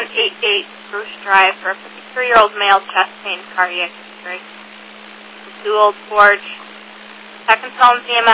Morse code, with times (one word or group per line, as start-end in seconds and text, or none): none